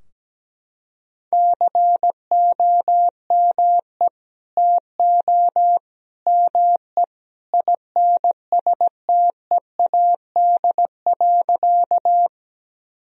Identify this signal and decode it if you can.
Morse code, heard as a keyed tone, about 17 wpm, 700 Hz. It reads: COME TO ME INSTEAD.
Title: Sentences from Top 1000 Words - Review Part 4 - 17wpm